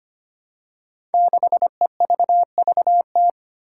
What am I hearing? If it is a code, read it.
Morse code, heard as a keyed tone, about 25 wpm, 700 Hz. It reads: 6EVVT
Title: Five Challenging Letter-Number Combinations 25wpm